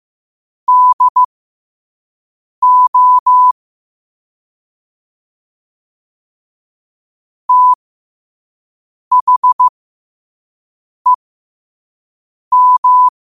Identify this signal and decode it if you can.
Morse code, heard as a keyed tone, about 15 wpm, spaced as for 5 wpm, 1000 Hz. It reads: DO THEM
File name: qotdfiles / 5wpm